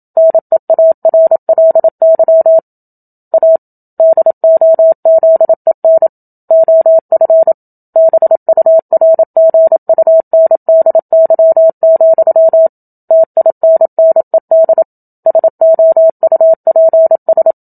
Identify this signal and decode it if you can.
Morse code, heard as a keyed tone, about 27 wpm, 650 Hz. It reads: NEARLY A DOZEN OF BURGUNDY, TINNED SOUPS
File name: Book Two / 01-Chapter 1 - 27 WPM